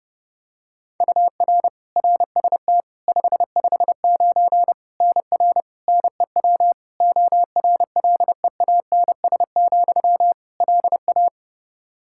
Morse code, heard as a keyed tone, about 32 wpm, 700 Hz. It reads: UR RST 559 NR NEW ORLEANS, LA
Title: K1USN SST QSO Practice 30wpm